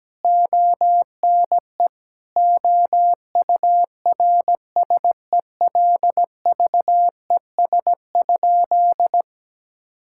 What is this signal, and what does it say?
Morse code, heard as a keyed tone, about 17 wpm, 700 Hz. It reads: ONE OURSELVES?
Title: Sentences from Top 1600 Words 17wpm